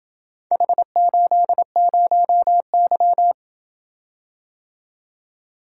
Morse code, heard as a keyed tone, about 27 wpm, 700 Hz. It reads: H80Y